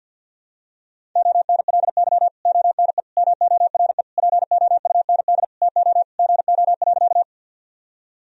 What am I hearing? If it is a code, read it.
Morse code, heard as a keyed tone, about 52 wpm, 700 Hz. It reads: ONLY ONE MORE POUND TO GO.
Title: Sentences from Top 1000 Words - Review Part 2 - 50wpm